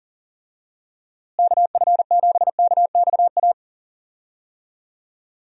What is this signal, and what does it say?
Morse code, heard as a keyed tone, about 40 wpm, 700 Hz. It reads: KF7KXA
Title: USA Call Signs - 6 Characters - Repeated 3x - 40wpm